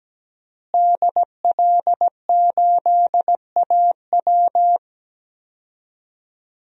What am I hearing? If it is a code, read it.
Morse code, heard as a keyed tone, about 17 wpm, 700 Hz. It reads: DL8AW
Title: DXCC-100 Call Signs - 5 Characters - Repeated 2x - 17wpm